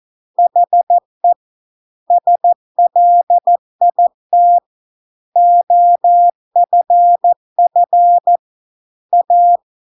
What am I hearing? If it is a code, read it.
Morse code, heard as a keyed tone, about 14 wpm, 700 Hz. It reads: HE SLIT OFF A